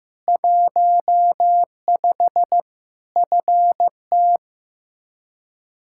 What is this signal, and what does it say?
Morse code, heard as a keyed tone, about 15 wpm, 700 Hz. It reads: 15 FT
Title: Practice QSO - Exchange 2 - Info - 15wpm